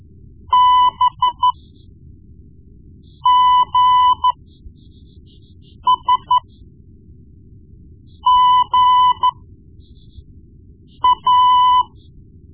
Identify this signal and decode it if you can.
Morse code, heard as a keyed tone, about 13 wpm, spaced as for 4 wpm, 1000 Hz. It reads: BGSGA
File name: Morse Lessons / Morse Lessons / Tape-1B